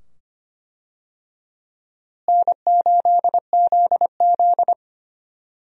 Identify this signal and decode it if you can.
Morse code, heard as a keyed tone, about 25 wpm, 700 Hz. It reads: N8ZZ
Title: USA Call Signs - 4 Characters - Repeated 2x - 25wpm